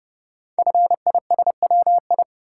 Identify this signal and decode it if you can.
Morse code, heard as a keyed tone, about 30 wpm, 700 Hz. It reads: FISWI
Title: Code Groups - 5 Letters - 30wpm